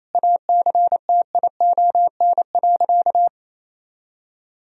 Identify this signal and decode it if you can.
Morse code, heard as a keyed tone, about 28 wpm, 700 Hz. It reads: ACTION.
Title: Sentences from Top 1000 Words - Review Part 4 - 28wpm